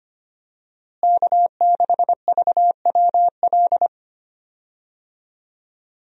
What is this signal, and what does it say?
Morse code, heard as a keyed tone, about 25 wpm, 700 Hz. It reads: K6VWL
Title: USA Call Signs - 5 Characters - Repeated 2x - 25wpm